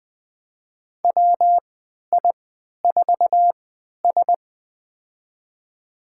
Morse code, heard as a keyed tone, about 20 wpm, spaced as for 11 wpm, 700 Hz. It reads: WI4S
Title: USA Call Signs - 4 Characters - 3x Character Spacing - 20wpm